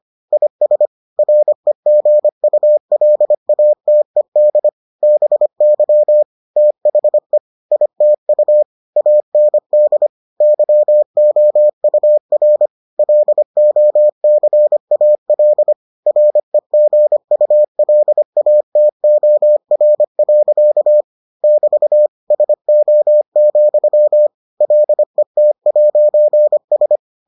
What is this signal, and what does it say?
Morse code, heard as a keyed tone, about 25 wpm, 600 Hz. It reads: IS REGULATED BY THE ITU AND YOUR LOCAL REGULATOR. <BT> SO, LET'S